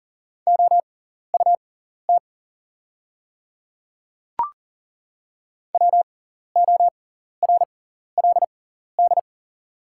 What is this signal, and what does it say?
Morse code, heard as a keyed tone, about 40 wpm, spaced as for 14 wpm, 700 Hz. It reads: OUT WORLD
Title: 6x Character Spacing - Top 200 Words - 40wpm